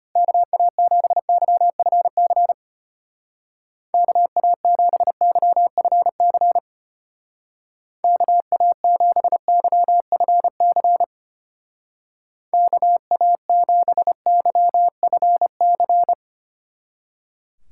Morse code, heard as a keyed tone, about 32 wpm, 700 Hz. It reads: KA7YFC KA7YFC KA7YFC KA7YFC